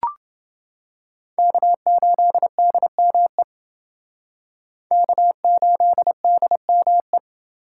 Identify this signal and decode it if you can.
Morse code, heard as a keyed tone, about 28 wpm, 700 Hz. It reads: K8DME K8DME